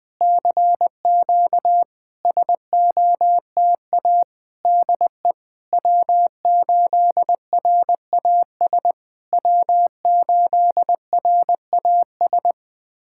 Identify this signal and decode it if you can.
Morse code, heard as a keyed tone, about 20 wpm, 700 Hz. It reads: CQ SOTA DE W8RAS W8RAS
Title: SOTA QSO Practice 20wpm